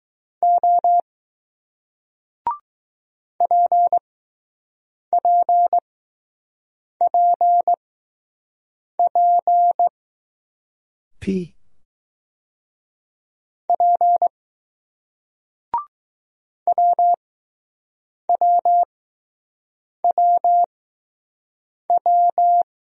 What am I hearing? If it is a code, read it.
Morse code, heard as a keyed tone, about 20 wpm, spaced as for 6 wpm, 700 Hz. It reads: O PPPP P WWWW